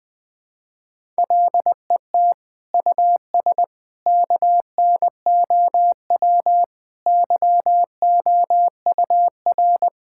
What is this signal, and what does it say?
Morse code, heard as a keyed tone, about 21 wpm, 700 Hz. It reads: LET US KNOW YOUR
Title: Sets of 9 Words - 20wpm